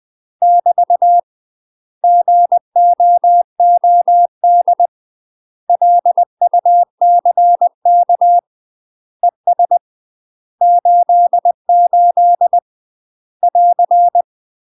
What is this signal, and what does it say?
Morse code, heard as a keyed tone, about 20 wpm, 700 Hz. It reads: <BT> GOOD LUCK ES 88 <AR>
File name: UK MASTER QSOs TRAINING / 20WPM / UK MASTER QSOs 0014